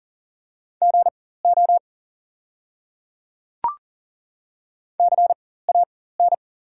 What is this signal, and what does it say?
Morse code, heard as a keyed tone, about 40 wpm, spaced as for 19 wpm, 690 Hz. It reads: GO CAN
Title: 4x Character Spacing - Top 100 Words - 40wpm